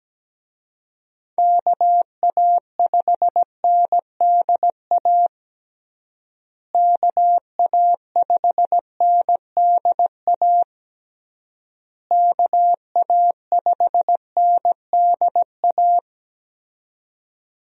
Morse code, heard as a keyed tone, about 17 wpm, 700 Hz. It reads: KA5NDA KA5NDA KA5NDA
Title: USA Call Signs - 6 Characters - Repeated 3x - 17wpm